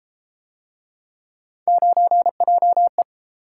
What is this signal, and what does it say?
Morse code, heard as a keyed tone, about 32 wpm, 700 Hz. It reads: 9JE